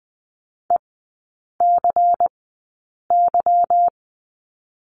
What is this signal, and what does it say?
Morse code, heard as a keyed tone, about 20 wpm, spaced as for 8 wpm, 700 Hz. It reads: ECY
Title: 178 Lesson 29e - Slash - Three Characters 20wpm